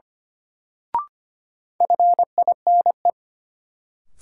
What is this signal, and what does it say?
Morse code, heard as a keyed tone, about 25 wpm, 700 Hz. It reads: FINE
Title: Sentences from Top 500 Words - New Word Drill - 25wpm